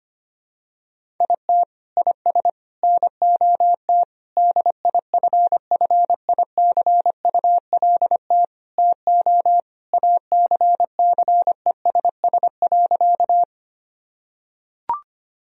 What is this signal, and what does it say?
Morse code, heard as a keyed tone, about 25 wpm, 700 Hz. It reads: IT IS NOT DIFFICULT TO ACCESS.
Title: Sentences from Top 1000 Words - Review Part 4 - 25wpm